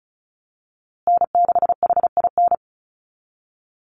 Morse code, heard as a keyed tone, about 34 wpm, 700 Hz. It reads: N6HIN
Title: USA Call Signs Repeated 2x 35wpm